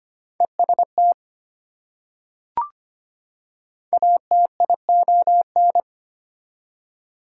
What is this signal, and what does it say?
Morse code, heard as a keyed tone, about 25 wpm, 700 Hz. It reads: EST ATION
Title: Prefixes and Suffixes 25wpm